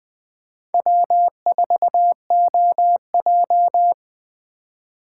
Morse code, heard as a keyed tone, about 20 wpm, 700 Hz. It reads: W4OJ
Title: Speed Racing - USA 1x2 Call Signs 17wpm